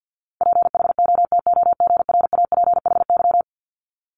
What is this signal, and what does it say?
Morse code, heard as a keyed tone, about 50 wpm, 690 Hz. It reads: PHOTOGRAPHY
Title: Multi-letter Phonemes Repeated 3x 50wpm